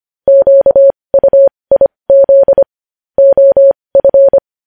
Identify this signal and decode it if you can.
Morse code, heard as a keyed tone, about 25 wpm, spaced as for 20 wpm, 550 Hz. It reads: QUIZ OF
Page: news headlines in morse code at 20 wpm